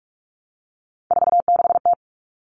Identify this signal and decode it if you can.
Morse code, heard as a keyed tone, about 44 wpm, 700 Hz. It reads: 46T
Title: All Cut Numbers - 3-Digit 45wpm